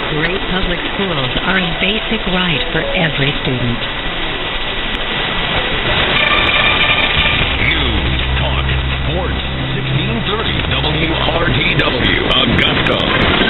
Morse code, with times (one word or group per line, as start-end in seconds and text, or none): none